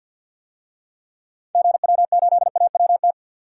Morse code, heard as a keyed tone, about 52 wpm, 700 Hz. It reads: GW8AWT
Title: DXCC-100 Call Signs - 6 Characters - 50wpm